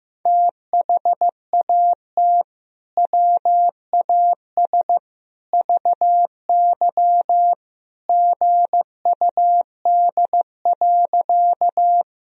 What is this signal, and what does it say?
Morse code, heard as a keyed tone, about 15 wpm, 700 Hz. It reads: THAT WAS VY GUD.
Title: Sentences from Top 200 Words and Abbreviations 15wpm